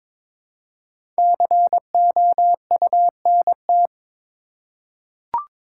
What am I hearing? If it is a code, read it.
Morse code, heard as a keyed tone, about 22 wpm, 700 Hz. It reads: COUNT
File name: 5 Letter Words - First 3 Letters Given - 22wpm